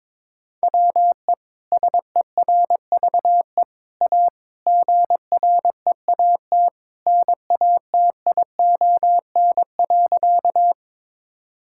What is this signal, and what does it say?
Morse code, heard as a keyed tone, about 22 wpm, 700 Hz. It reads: WE SERVE A GREAT NATION.
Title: Sentences from Top 1000 Words - Review Part 6 - 22wpm